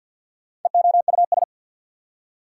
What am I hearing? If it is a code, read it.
Morse code, heard as a keyed tone, about 53 wpm, 690 Hz. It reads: EOUS
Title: Prefixes and Suffixes 50wpm